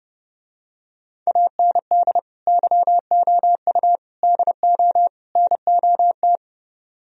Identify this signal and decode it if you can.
Morse code, heard as a keyed tone, about 32 wpm, 700 Hz. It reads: AND YOU DO NOT